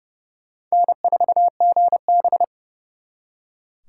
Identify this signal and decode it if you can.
Morse code, heard as a keyed tone, about 37 wpm, 700 Hz. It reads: N4GB
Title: Speed Racing - USA 1x2 Call Signs 30wpm